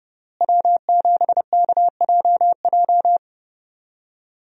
Morse code, heard as a keyed tone, about 30 wpm, 700 Hz. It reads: W7KJJ
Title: USA Call Signs - 5 Characters - Repeated 2x - 30wpm